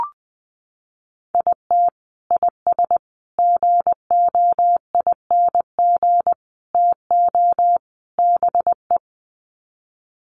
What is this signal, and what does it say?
Morse code, heard as a keyed tone, about 20 wpm, 700 Hz. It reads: IT IS GOING TO BE
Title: Sets of 5 Words 20wpm